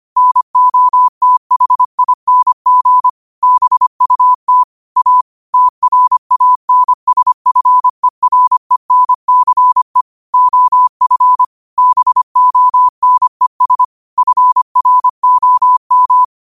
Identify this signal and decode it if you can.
Morse code, heard as a keyed tone, about 25 wpm, 1000 Hz. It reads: NOTHING BUT A TRANSFERENCE OF BONES FROM
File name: qotdfiles / 25wpm